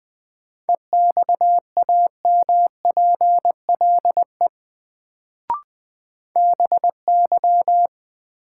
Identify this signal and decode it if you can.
Morse code, heard as a keyed tone, about 20 wpm, 700 Hz. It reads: EXAMPLE BY
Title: Sets of 2 Words - New Word Drill - 20wpm